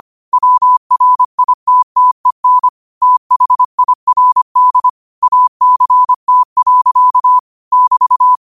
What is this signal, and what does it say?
Morse code, heard as a keyed tone, about 25 wpm, 1000 Hz. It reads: WRITTEN THIRD ACT. <BT>